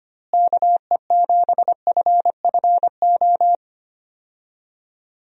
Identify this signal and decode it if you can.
Morse code, heard as a keyed tone, about 25 wpm, 700 Hz. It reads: KE7FFO
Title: USA Call Signs - 6 Characters - Repeated 3x - 25wpm